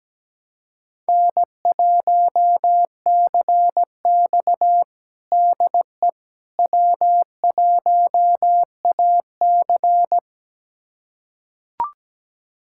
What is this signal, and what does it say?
Morse code, heard as a keyed tone, about 17 wpm, 700 Hz. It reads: N1CX DE W1AC
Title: USA Callsign DE Callsign - 4 Characters - 17wpm